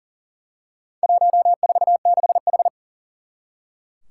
Morse code, heard as a keyed tone, about 40 wpm, 700 Hz. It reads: 146H
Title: Four Challenging Letter-Number Combinations 40wpm